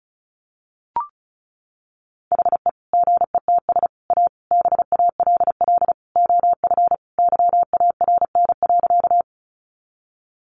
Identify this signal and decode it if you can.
Morse code, heard as a keyed tone, about 34 wpm, 700 Hz. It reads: HE GETS A BALL OF YARN.
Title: Additional Easy Phrases 35wpm